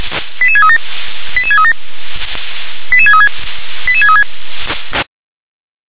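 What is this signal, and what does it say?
Five-tone selective call (ZVEI), heard as quick runs of five tones, tones of about 70 ms. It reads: 88637 88637 88636 88636
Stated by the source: Received by SDR.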